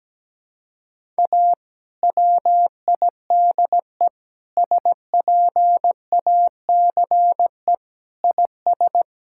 Morse code, heard as a keyed tone, about 17 wpm, 700 Hz. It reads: A WIDE SPACE IS
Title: Sentences from Top 1000 Words - Review Part 6 - 17wpm